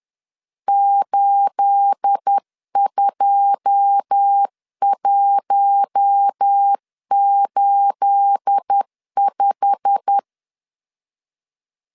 Morse code, heard as a keyed tone, about 10 wpm, 800 Hz. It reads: 82185
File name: LEZIONI / 9 28 10wpm 2842